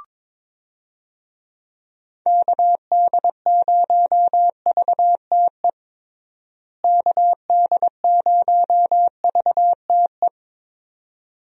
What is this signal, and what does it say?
Morse code, heard as a keyed tone, about 21 wpm, 700 Hz. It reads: KD0VTE KD0VTE
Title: USA Call Signs - 6 Characters - Repeated 3x - 22wpm